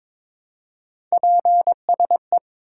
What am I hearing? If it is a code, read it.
Morse code, heard as a keyed tone, about 22 wpm, 700 Hz. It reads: PSE
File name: Top 90 QSO Elements - 3x Character Spacing - 22wpm